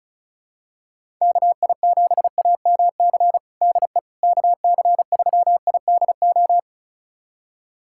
Morse code, heard as a keyed tone, about 34 wpm, 700 Hz. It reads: KI7AMC DE KC3IDO
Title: USA Callsign DE Callsign 35wpm